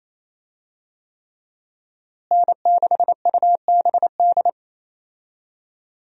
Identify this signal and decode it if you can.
Morse code, heard as a keyed tone, about 27 wpm, 700 Hz. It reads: N6UBD
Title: USA Call Signs - 5 Characters - Repeated 3x - 28wpm